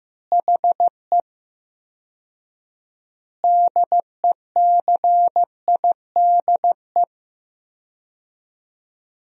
Morse code, heard as a keyed tone, about 16 wpm, 700 Hz. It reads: HE DECIDE
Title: 4x Word-Spacing - Sentences from Top 500 Words - 15wpm